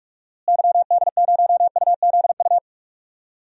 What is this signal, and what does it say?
Morse code, heard as a keyed tone, about 46 wpm, 700 Hz. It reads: YD0UZU
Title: DXCC-100 Call Signs - 6 Characters - Repeated 3x - 45wpm